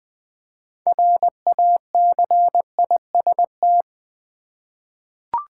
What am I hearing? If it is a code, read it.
Morse code, heard as a keyed tone, about 20 wpm, 700 Hz. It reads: RACIST
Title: 6 Letter Words - First 3 Letters Given - 20wpm